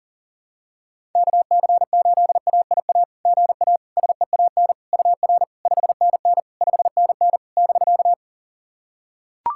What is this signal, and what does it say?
Morse code, heard as a keyed tone, about 40 wpm, 690 Hz. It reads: KC8AIA GA SEAN UR 5NN 5NN <BK>
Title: SOTA QSO Practice 40wpm